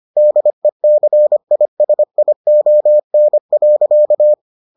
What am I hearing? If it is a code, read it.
Morse code, heard as a keyed tone, about 25 wpm, 600 Hz. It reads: DECISION.